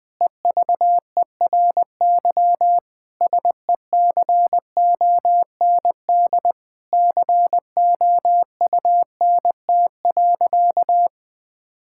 Morse code, heard as a keyed tone, about 20 wpm, 700 Hz. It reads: EVERY SECOND COUNT.
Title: Difficult Sentences with Hints - 20wpm